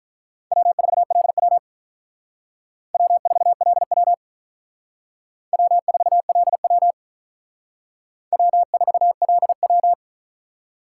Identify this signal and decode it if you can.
Morse code, heard as a keyed tone, about 46 wpm, 700 Hz. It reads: W4LW W4LW W4LW W4LW